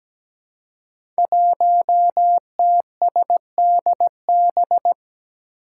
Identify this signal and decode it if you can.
Morse code, heard as a keyed tone, about 17 wpm, 700 Hz. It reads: 1TSDB
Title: Five Challenging Letter-Number Combinations 17wpm